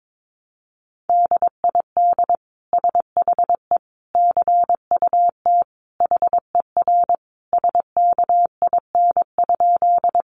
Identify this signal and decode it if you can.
Morse code, heard as a keyed tone, about 22 wpm, 700 Hz. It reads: DID SHE CUT HER SKIN?
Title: Sentences from Top 1300 Words 22wpm